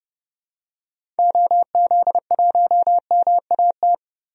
Morse code, heard as a keyed tone, about 30 wpm, 700 Hz. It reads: OZ1MAT
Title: DXCC-100 Call Signs - 6 Characters - 30wpm